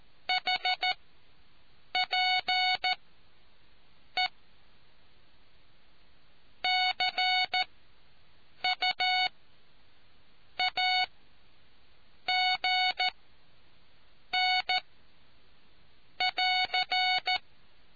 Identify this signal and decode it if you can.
Morse code, heard as a keyed tone, about 14 wpm, spaced as for 6 wpm, 740 Hz. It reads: HPE CUAGN<AR>